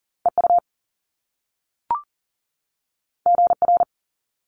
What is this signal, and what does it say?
Morse code, heard as a keyed tone, about 40 wpm, 700 Hz. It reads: EU GR
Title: Top Two Letter Combinations in Words 40wpm